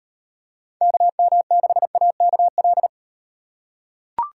Morse code, heard as a keyed tone, about 38 wpm, 700 Hz. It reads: KM6AKL